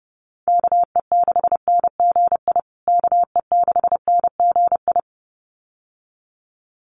Morse code, heard as a keyed tone, about 30 wpm, 700 Hz. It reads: KE6NGI KE6NGI
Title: USA Call Signs Repeated 2x 30wpm